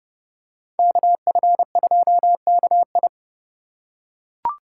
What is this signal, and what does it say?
Morse code, heard as a keyed tone, about 29 wpm, 700 Hz. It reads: KF2KI